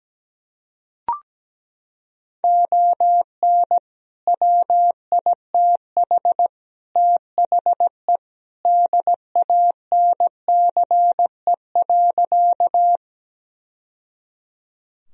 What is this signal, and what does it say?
Morse code, heard as a keyed tone, about 17 wpm, 700 Hz. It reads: ON WITH THE DANCE.
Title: Sentences from Top 1000 Words - Review Part 4 - 17wpm